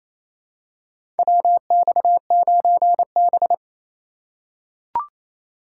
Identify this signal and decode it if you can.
Morse code, heard as a keyed tone, about 28 wpm, 700 Hz. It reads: WX9B
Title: USA Call Signs - 4 Characters - 3x Character Spacing - 28wpm